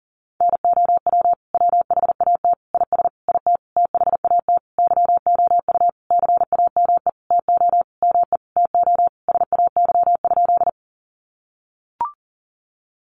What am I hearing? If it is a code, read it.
Morse code, heard as a keyed tone, about 40 wpm, 700 Hz. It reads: NOW WHAT IS IT THAT YOU CAME TO ME TO SAY?
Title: Sentences from Top 200 Words 40wpm